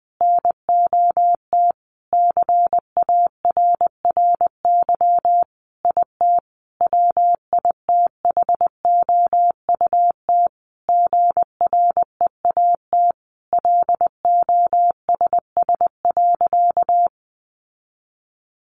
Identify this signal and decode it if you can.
Morse code, heard as a keyed tone, about 20 wpm, 700 Hz. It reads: NOT CARRY IT WITHOUT GREAT LOSS.
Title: Sentences from Top 1400 Words 20wpm